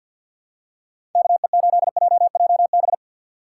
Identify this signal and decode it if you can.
Morse code, heard as a keyed tone, about 48 wpm, 700 Hz. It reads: KE8JJB
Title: USA Call Signs - 6 Characters - 3x Character Spacing - 50wpm